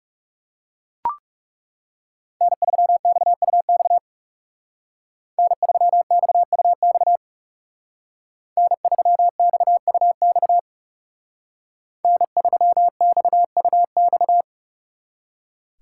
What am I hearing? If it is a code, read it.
Morse code, heard as a keyed tone, about 39 wpm, 700 Hz. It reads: N3XUX N3XUX N3XUX N3XUX